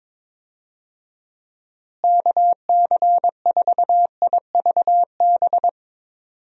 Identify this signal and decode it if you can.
Morse code, heard as a keyed tone, about 22 wpm, 700 Hz. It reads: KC4IVB
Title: USA Call Signs - 6 Characters - Repeated 3x - 22wpm